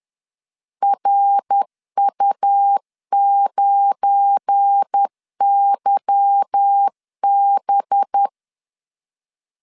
Morse code, heard as a keyed tone, about 10 wpm, 800 Hz. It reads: RU9YB
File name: LEZIONI / 10 SIGNS / 10A SLASH 10wpm 2808